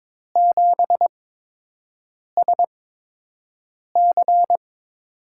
Morse code, heard as a keyed tone, about 23 wpm, spaced as for 6 wpm, 700 Hz. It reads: 7SC
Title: USA Call Signs - 6 Characters - 8x Character Spacing - 22wpm